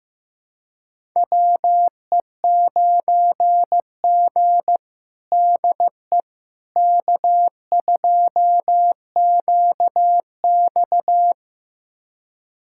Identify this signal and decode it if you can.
Morse code, heard as a keyed tone, about 16 wpm, 700 Hz. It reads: WE9G DE K2QX
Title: USA Callsign DE Callsign - 4 Characters - 15wpm